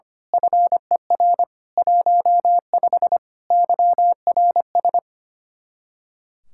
Morse code, heard as a keyed tone, about 24 wpm, 700 Hz. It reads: FER 15 YRS
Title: Speed-Racing Common QSO Phrases - 25wpm